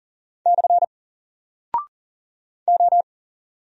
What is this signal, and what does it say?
Morse code, heard as a keyed tone, about 40 wpm, 700 Hz. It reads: / O
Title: Single Character (Letter:Number:Punctuation:Prosign) 40wpm